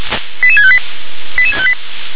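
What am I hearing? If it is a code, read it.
Five-tone selective call (ZVEI), heard as quick runs of five tones, tones of about 70 ms. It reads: 88658 88658
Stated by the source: ZVEI five-tone selective call received on SDR.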